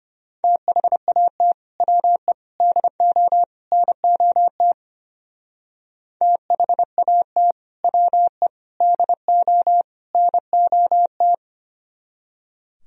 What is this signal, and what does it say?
Morse code, heard as a keyed tone, about 31 wpm, 700 Hz. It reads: THAT WE DO NOT THAT WE DO NOT
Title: Speed-Racing - Sets of 4 Words 25wpm